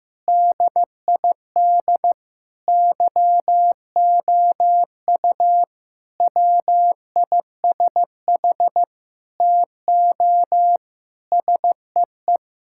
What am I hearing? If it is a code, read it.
Morse code, heard as a keyed tone, about 15 wpm, 700 Hz. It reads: DID YOU WISH TO SEE
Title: Sentences from Top 1600 Words 15wpm